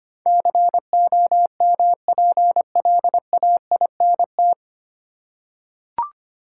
Morse code, heard as a keyed tone, about 25 wpm, 700 Hz. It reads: COMPLAINT
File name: Nine Letter Words - 25wpm